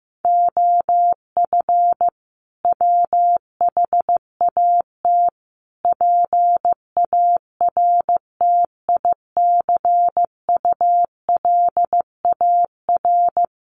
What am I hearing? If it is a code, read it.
Morse code, heard as a keyed tone, about 15 wpm, 700 Hz. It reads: OF WHAT PARTICULAR
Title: Sentences from Top 1300 Words 15wpm